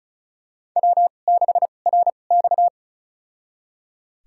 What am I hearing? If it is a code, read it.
Morse code, heard as a keyed tone, about 36 wpm, spaced as for 25 wpm, 700 Hz. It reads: W6RX
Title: USA Call Signs - 4 Characters - 2x Character Spacing - 35wpm